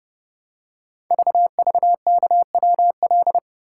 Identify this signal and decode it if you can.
Morse code, heard as a keyed tone, about 30 wpm, 700 Hz. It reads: VVKWL
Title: Code Groups - 5 Letters - 30wpm